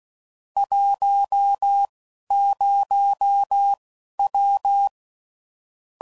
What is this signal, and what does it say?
Morse code, heard as a keyed tone, about 16 wpm, spaced as for 11 wpm, 800 Hz. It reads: 10W
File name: CW Code Groups @10WPM / QSOs / QSO kc7sac